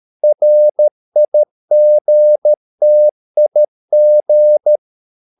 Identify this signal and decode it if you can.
Morse code, heard as a keyed tone, about 13 wpm, 600 Hz. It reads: RIGTIG
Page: Ferie 13 wpm